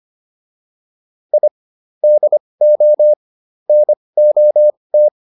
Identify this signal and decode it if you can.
Morse code, heard as a keyed tone, about 25 wpm, spaced as for 20 wpm, 600 Hz. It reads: I DO NOT